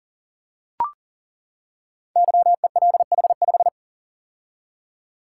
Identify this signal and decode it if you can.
Morse code, heard as a keyed tone, about 42 wpm, 690 Hz. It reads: YELH5
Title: Code Groups - 5 Letter:Numbers - 40wpm